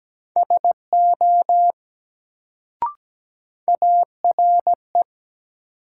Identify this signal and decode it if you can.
Morse code, heard as a keyed tone, about 17 wpm, 700 Hz. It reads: SO ARE
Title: Sentences from Top 100 Words - New Word Drill - 17wpm